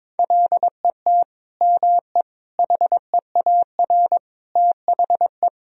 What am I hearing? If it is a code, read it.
Morse code, heard as a keyed tone, about 21 wpm, 700 Hz. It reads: LET ME HEAR THE
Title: Sentences from Top 1000 Words - Review Part 1 - 22wpm